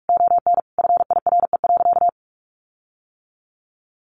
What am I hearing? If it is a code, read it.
Morse code, heard as a keyed tone, about 44 wpm, 700 Hz. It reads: ON FIRE.
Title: Sentences from Top 1400 Words 45wpm